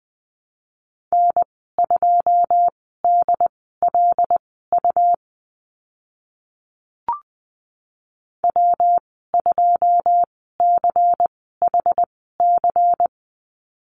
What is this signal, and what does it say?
Morse code, heard as a keyed tone, about 20 wpm, spaced as for 14 wpm, 700 Hz. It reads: N2DLU W2CHC
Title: 140 Lesson 23g - Letter f - Call Signs 20wpm